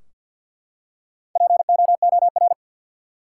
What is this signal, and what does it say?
Morse code, heard as a keyed tone, about 52 wpm, 700 Hz. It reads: POOR
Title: Four Letter Words - 50wpm